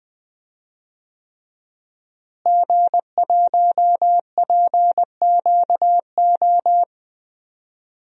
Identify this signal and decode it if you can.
Morse code, heard as a keyed tone, about 20 wpm, 700 Hz. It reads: G1PQO
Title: DXCC-100 Call Signs - 5 Characters - Repeated 3x - 20wpm